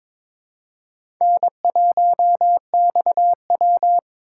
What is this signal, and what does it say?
Morse code, heard as a keyed tone, about 22 wpm, 700 Hz. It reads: N1XW
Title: USA Call Signs - 4 Characters - 4x Character Spacing - 22wpm